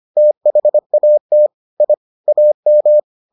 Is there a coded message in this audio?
Yes: Morse code from 0.2 to 3.0 s, heard as a keyed tone, about 25 wpm, 600 Hz. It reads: THAT I AM